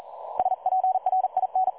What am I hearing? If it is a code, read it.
Morse code, heard as a keyed tone, about 41 wpm, 750 Hz. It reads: SPRIN
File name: dlazaawansowanych / nagraniazpasma / websdr recording 3565.0kHz